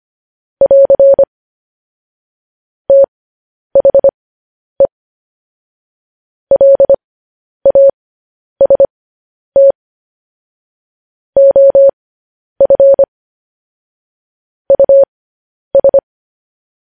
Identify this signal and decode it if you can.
Morse code, heard as a keyed tone, about 25 wpm, spaced as for 10 wpm, 550 Hz. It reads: <AR> THE LAST OF US
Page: News Headlines in Morse Code at 10 WPM